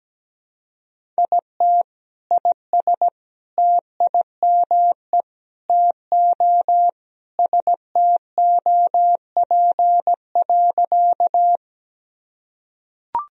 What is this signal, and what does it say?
Morse code, heard as a keyed tone, about 17 wpm, 700 Hz. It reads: IT IS TIME TO STOP.
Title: Sentences from Top 1000 Words - Review Part 1 - 17wpm